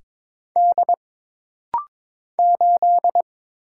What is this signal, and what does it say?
Morse code, heard as a keyed tone, about 22 wpm, 700 Hz. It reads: D 8
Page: This rapid-fire timing is geared to push your ICR (Instant Character Recognition) skills.